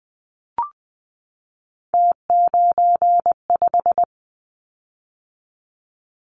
Morse code, heard as a keyed tone, about 19 wpm, 700 Hz. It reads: T95
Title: Callsign Prefixes for Top 50 DXCC - 20wpm